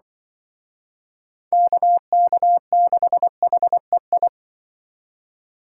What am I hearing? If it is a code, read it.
Morse code, heard as a keyed tone, about 23 wpm, 700 Hz. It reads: KK6HEI
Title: Speed Racing - USA Call Signs 20wpm